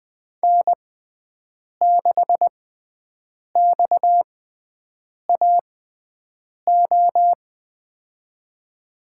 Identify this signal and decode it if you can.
Morse code, heard as a keyed tone, about 20 wpm, spaced as for 7 wpm, 700 Hz. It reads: N6XAO
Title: USA Call Signs - 5 Characters - 6x Character Spacing - 20wpm